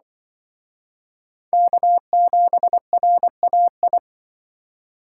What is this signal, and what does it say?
Morse code, heard as a keyed tone, about 25 wpm, 700 Hz. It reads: K7RAI